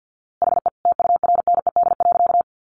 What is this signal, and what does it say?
Morse code, heard as a keyed tone, about 50 wpm, 700 Hz. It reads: HE TURNED.
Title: A Sound Of Thunder 50wpm